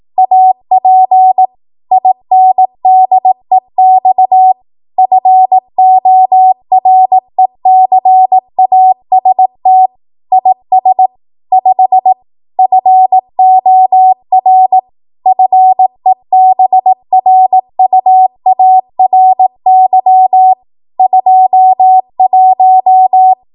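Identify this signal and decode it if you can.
Morse code, heard as a keyed tone, about 18 wpm, 750 Hz. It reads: AP INDEX FORECAST IS 5 FOR FEBRUARY 21
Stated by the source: Code practice at 18 WPM, a transition run bridging 15 to 20 WPM.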